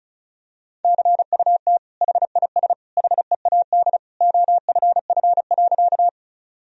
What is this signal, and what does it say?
Morse code, heard as a keyed tone, about 36 wpm, 700 Hz. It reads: CUT HIS HEAD OFF.
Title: Sentences from Top 1000 Words - Review Part 1 - 35wpm